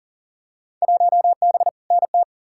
Morse code, heard as a keyed tone, about 40 wpm, 700 Hz. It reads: 1B NT
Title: Field Day Exchanges Encore - 40wpm